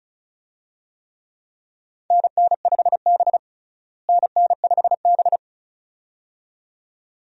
Morse code, heard as a keyed tone, about 36 wpm, 700 Hz. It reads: NN5B NN5B
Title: USA Call Signs - 4 Characters - Repeated 2x - 35wpm